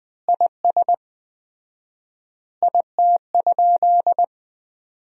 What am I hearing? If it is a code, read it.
Morse code, heard as a keyed tone, about 21 wpm, 700 Hz. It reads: IS IT?
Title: 4x Word-Spacing - Sentences from Top 100 Words - 20wpm